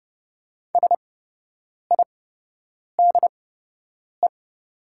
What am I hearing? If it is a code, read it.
Morse code, heard as a keyed tone, about 30 wpm, spaced as for 8 wpm, 700 Hz. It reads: SIDE
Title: 8x Character Spacing - Top 200 Words - 30wpm